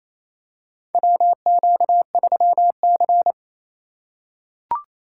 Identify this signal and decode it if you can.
Morse code, heard as a keyed tone, about 28 wpm, 700 Hz. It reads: WQ3C
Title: USA Call Signs - 4 Characters - 3x Character Spacing - 28wpm